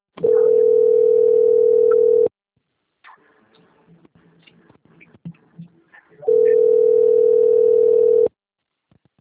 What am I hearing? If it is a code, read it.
Ringback tone (North America), 440 + 480 Hz.